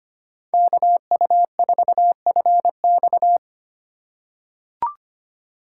Morse code, heard as a keyed tone, about 25 wpm, 700 Hz. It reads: KU4FX
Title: USA Call Signs - 5 Characters - 8x Character Spacing - 25wpm